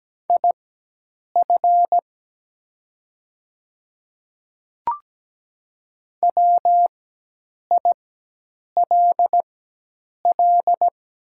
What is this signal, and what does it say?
Morse code, heard as a keyed tone, about 17 wpm, spaced as for 8 wpm, 700 Hz. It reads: IF WILL
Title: 4x Character Spacing - Top 100 Words - 17wpm